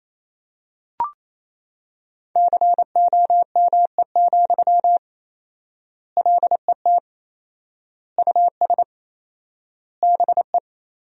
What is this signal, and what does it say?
Morse code, heard as a keyed tone, about 28 wpm, 700 Hz. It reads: COME, LET US BE
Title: 4x Word-Spacing - Sentences from Top 300 Words - 28wpm